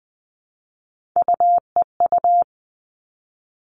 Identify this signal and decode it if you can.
Morse code, heard as a keyed tone, about 20 wpm, 700 Hz. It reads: UEU
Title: All Cut Numbers - 3-Digit 20wpm